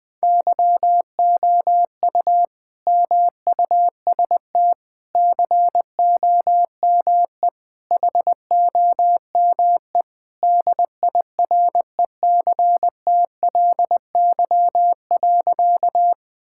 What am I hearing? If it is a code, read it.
Morse code, heard as a keyed tone, about 20 wpm, 700 Hz. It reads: YOU MUST COME HOME DIRECTLY.
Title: Sentences from Top 1000 Words - Review Part 5 - 20wpm